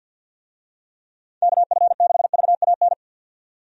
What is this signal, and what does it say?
Morse code, heard as a keyed tone, about 52 wpm, 700 Hz. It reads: KF6VAN
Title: USA Call Signs - 6 Characters - 50wpm